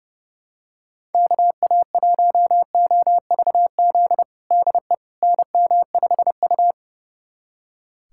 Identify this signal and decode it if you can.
Morse code, heard as a keyed tone, about 32 wpm, 700 Hz. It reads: KA1OVZ DE NM5U
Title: USA Callsign DE Callsign 30wpm